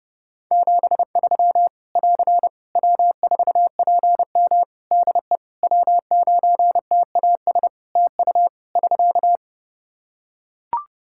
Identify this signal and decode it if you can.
Morse code, heard as a keyed tone, about 32 wpm, 700 Hz. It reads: 73 <AR> W4PM DE W9TAS TU <SK>
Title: K1USN SST QSO Practice 30wpm